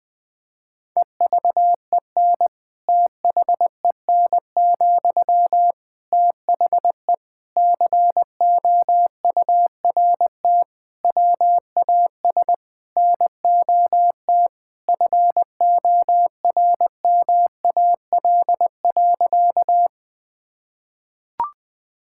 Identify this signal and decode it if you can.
Morse code, heard as a keyed tone, about 19 wpm, 700 Hz. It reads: EVEN THEN, THE COURT WAS NOT FORMAL.
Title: Sentences from Top 1600 Words 20wpm